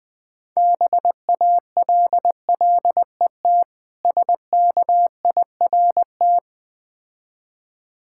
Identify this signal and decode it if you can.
Morse code, heard as a keyed tone, about 21 wpm, 690 Hz. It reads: BALLET SKIRT
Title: Difficult Words - Speed-Racing with Hints - 20wpm